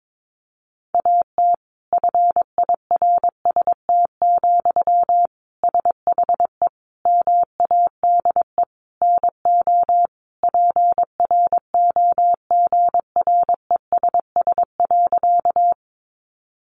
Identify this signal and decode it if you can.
Morse code, heard as a keyed tone, about 22 wpm, 700 Hz. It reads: AT FIRST, SHE MADE NO PROGRESS.
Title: Sentences from Top 1400 Words 22wpm